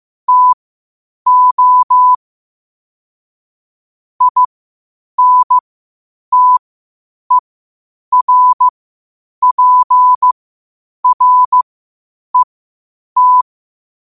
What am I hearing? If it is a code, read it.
Morse code, heard as a keyed tone, about 15 wpm, spaced as for 8 wpm, 1000 Hz. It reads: TO INTERPRET
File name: qotdfiles / 8wpm